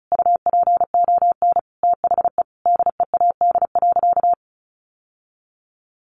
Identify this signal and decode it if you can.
Morse code, heard as a keyed tone, about 34 wpm, 700 Hz. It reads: UPON THE DEAD.
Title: Sentences from Top 1500 Words 35wpm